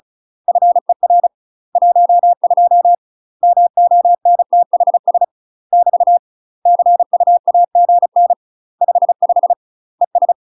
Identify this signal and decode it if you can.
Morse code, heard as a keyed tone, about 34 wpm, 700 Hz. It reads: FER 12 MONTHS <BT> CUAGN 55 ES
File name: UK MASTER QSOs TRAINING/35WPM